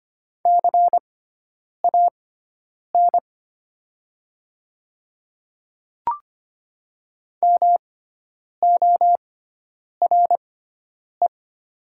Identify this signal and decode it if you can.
Morse code, heard as a keyed tone, about 25 wpm, spaced as for 9 wpm, 700 Hz. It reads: CAN MORE
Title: 6x Character Spacing - Top 100 Words - 25wpm